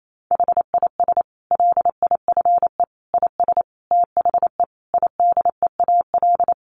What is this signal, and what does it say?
Morse code, heard as a keyed tone, about 28 wpm, 700 Hz. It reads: HIS LIFE IS THE IDEAL
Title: Sentences from Top 1400 Words 28wpm